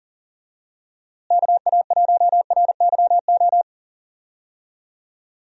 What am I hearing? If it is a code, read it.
Morse code, heard as a keyed tone, about 40 wpm, 700 Hz. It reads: KA1RYO
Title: USA Call Signs - 6 Characters - Repeated 3x - 40wpm